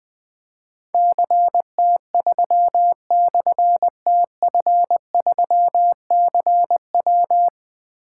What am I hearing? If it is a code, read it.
Morse code, heard as a keyed tone, about 21 wpm, 700 Hz. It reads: CT3/TF3CW
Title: Difficult Call Signs 20wpm